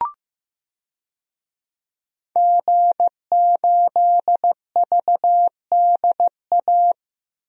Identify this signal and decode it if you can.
Morse code, heard as a keyed tone, about 15 wpm, 700 Hz. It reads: G8VDA
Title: DXCC-100 Call Signs - 5 Characters - 15wpm